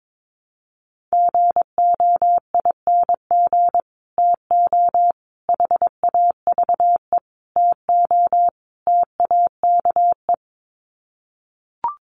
Morse code, heard as a keyed tone, about 22 wpm, 700 Hz. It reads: GOING TO HAVE TO TAKE
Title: Sets of 5 Words Encore 22wpm